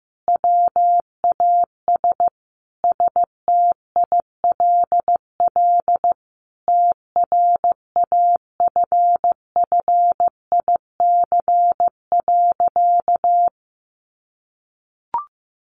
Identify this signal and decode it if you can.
Morse code, heard as a keyed tone, about 15 wpm, 700 Hz. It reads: WAS STILL TRAFFIC.
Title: Sentences from Top 1300 Words 15wpm